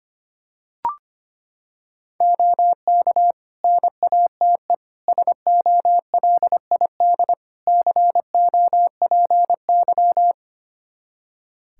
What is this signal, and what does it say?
Morse code, heard as a keyed tone, about 25 wpm, 700 Hz. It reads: OK NATE SOLID COPY
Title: Practice QSO - Exchange 3 - More Info - 25wpm